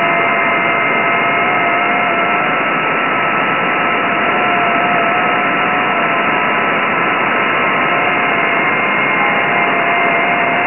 FT8.